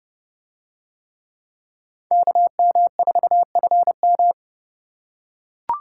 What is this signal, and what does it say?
Morse code, heard as a keyed tone, about 30 wpm, 700 Hz. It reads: KM4FM